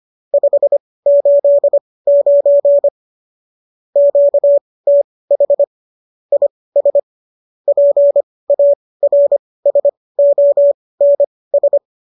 Morse code, heard as a keyed tone, about 25 wpm, spaced as for 18 wpm, 570 Hz. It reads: 589 QTH IS PARSONS
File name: qso205 18